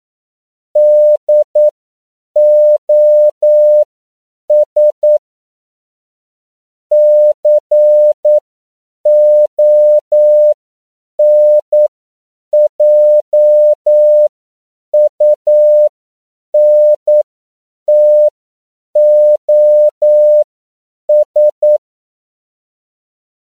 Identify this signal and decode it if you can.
Morse code, heard as a keyed tone, about 9 wpm, spaced as for 7 wpm, 600 Hz. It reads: DOS CONJUNTOS
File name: CW / 10PPM